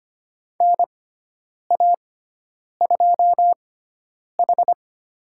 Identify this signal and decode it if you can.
Morse code, heard as a keyed tone, about 25 wpm, spaced as for 9 wpm, 700 Hz. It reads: NA2H
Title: USA Call Signs - 4 Characters - 6x Character Spacing - 25wpm